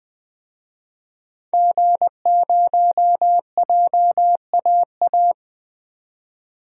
Morse code, heard as a keyed tone, about 20 wpm, 700 Hz. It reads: G0JAA
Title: DXCC-100 Call Signs - 5 Characters - 20wpm